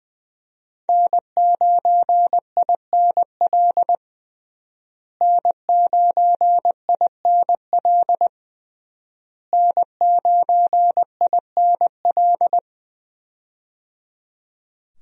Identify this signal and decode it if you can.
Morse code, heard as a keyed tone, about 21 wpm, 700 Hz. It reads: N9INL N9INL N9INL
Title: USA Call Signs - 5 Characters - Repeated 3x - 20wpm